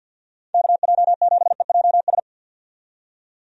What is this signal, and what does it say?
Morse code, heard as a keyed tone, about 52 wpm, 700 Hz. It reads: KJ7EJS
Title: USA Call Signs - 6 Characters - 50wpm